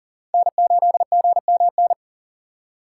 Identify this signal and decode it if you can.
Morse code, heard as a keyed tone, about 38 wpm, 700 Hz. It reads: N8GMN
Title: USA Call Signs - 6x Character Spacing - 40wpm